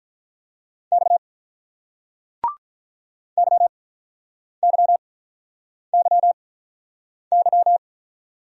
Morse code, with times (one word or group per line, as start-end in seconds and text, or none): 0.9-1.2 X
3.4-7.8 YYYY